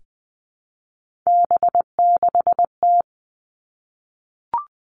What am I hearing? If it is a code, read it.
Morse code, heard as a keyed tone, about 20 wpm, 700 Hz. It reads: B6T
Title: Three Challenging Letter-Number Combinations 20wpm